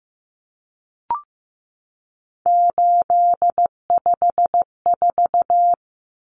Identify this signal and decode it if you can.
Morse code, heard as a keyed tone, about 15 wpm, 700 Hz. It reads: 854
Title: Common Cut Numbers - 3-Digit 15wpm